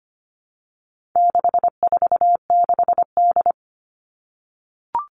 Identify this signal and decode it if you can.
Morse code, heard as a keyed tone, about 25 wpm, 700 Hz. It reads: 646D
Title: All Cut Numbers - 4-Digit 25wpm